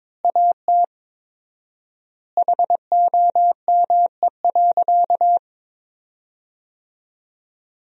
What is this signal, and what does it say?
Morse code, heard as a keyed tone, about 22 wpm, 700 Hz. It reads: AT HOME.
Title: 4x Word-Spacing - Sentences from Top 400 Words - 22wpm